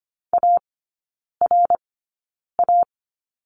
Morse code, heard as a keyed tone, about 25 wpm, spaced as for 9 wpm, 700 Hz. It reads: ARA
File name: course / 25:10wpm / 093 Lesson 16e - Letter c - Three Characters 25wpm